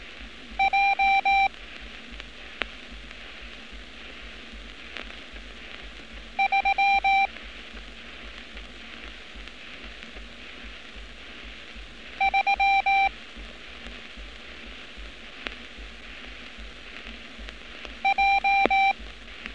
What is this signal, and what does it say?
Morse code, heard as a keyed tone, about 18 wpm, 780 Hz. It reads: J 3 3 J
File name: training / NAVPERS 11200-11211 / NAVPERS 11206 RA RB